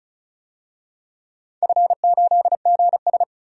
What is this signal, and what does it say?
Morse code, heard as a keyed tone, about 34 wpm, 700 Hz. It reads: F8GS
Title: DXCC-100 Call Signs - 4 Characters - 35wpm